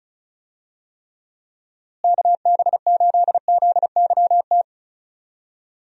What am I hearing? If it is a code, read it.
Morse code, heard as a keyed tone, about 36 wpm, 700 Hz. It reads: KB8ZYT
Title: USA Call Signs - 6 Characters - Repeated 3x - 35wpm